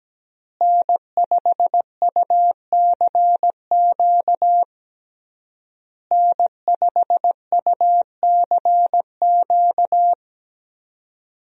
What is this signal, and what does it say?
Morse code, heard as a keyed tone, about 17 wpm, 700 Hz. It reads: N5UCQ N5UCQ
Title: USA Call Signs - 5 Characters - Repeated 3x - 17wpm